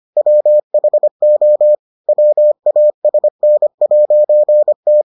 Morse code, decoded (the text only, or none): WHO WASN'T